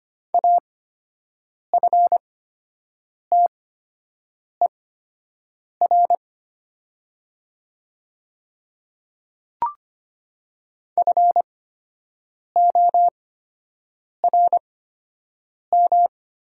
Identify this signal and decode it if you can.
Morse code, heard as a keyed tone, about 25 wpm, spaced as for 7 wpm, 700 Hz. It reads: AFTER FORM
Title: 8x Character Spacing - Top 300 Words - 25wpm